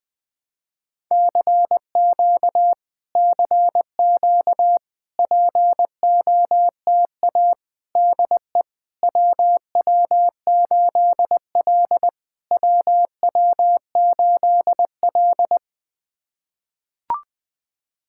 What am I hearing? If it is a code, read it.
Morse code, heard as a keyed tone, about 20 wpm, 700 Hz. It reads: CQ CQ POTA DE WW8L WW8L
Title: POTA QSO Practice - 20wpm